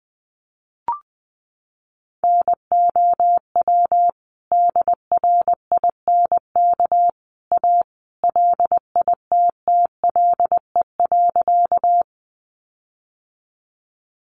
Morse code, heard as a keyed tone, about 19 wpm, 700 Hz. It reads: NOW DRINK A LITTLE.
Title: Sentences from Top 1500 Words 20wpm